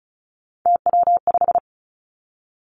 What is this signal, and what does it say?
Morse code, heard as a keyed tone, about 36 wpm, 700 Hz. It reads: TW5
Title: Three Challenging Letter-Number Combinations 35wpm